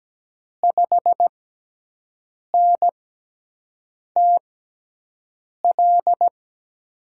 Morse code, heard as a keyed tone, about 17 wpm, spaced as for 6 wpm, 700 Hz. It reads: 5NTL
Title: USA Call Signs - 6 Characters - 6x Character Spacing - 17wpm